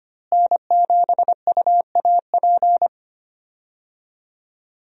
Morse code, heard as a keyed tone, about 25 wpm, 700 Hz. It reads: N7UAP
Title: USA Call Signs - 5 Characters - Repeated 3x - 25wpm